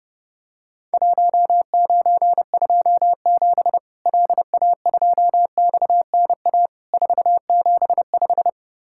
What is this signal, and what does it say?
Morse code, heard as a keyed tone, about 30 wpm, 700 Hz. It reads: 1927 LA2XNA 475
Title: Worked All Europe DX Contest - QTC Exchange - 30wpm